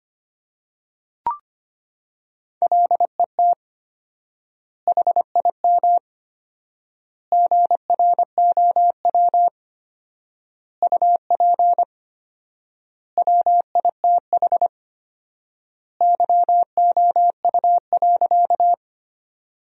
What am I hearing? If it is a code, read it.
Morse code, heard as a keyed tone, about 25 wpm, 700 Hz. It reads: LET HIM GROW UP WITH YOU.
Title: 4x Word-Spacing - Sentences from Top 300 Words - 25wpm